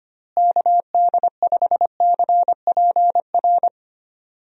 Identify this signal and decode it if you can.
Morse code, heard as a keyed tone, about 25 wpm, 700 Hz. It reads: KD5CPR
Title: USA Call Signs - 6 Characters - Repeated 2x - 25wpm